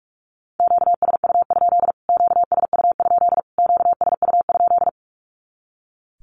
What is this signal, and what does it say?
Morse code, heard as a keyed tone, about 44 wpm, 700 Hz. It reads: QSU? QSU? QSU?
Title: Q-Codes Repeated 3x 45wpm